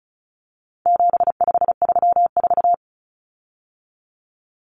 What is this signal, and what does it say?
Morse code, heard as a keyed tone, about 34 wpm, 700 Hz. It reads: 7534
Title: Common Cut Numbers - 4-Digit 35wpm